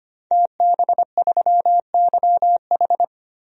Morse code, heard as a keyed tone, about 25 wpm, 700 Hz. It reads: TB3YH
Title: DXCC-100 Call Signs - 5 Characters - 25wpm